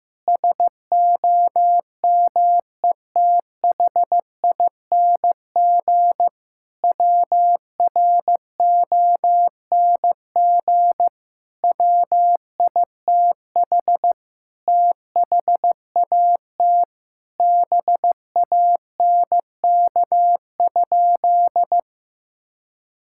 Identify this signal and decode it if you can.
Morse code, heard as a keyed tone, about 14 wpm, 700 Hz. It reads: SOMETHING WRONG WITH THAT BANK?
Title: Sentences from Top 1000 Words - Review Part 6 - 15wpm